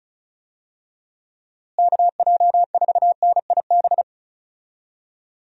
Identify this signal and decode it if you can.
Morse code, heard as a keyed tone, about 34 wpm, 700 Hz. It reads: KJ4NIB
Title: USA Call Signs - 6 Characters - Repeated 3x - 35wpm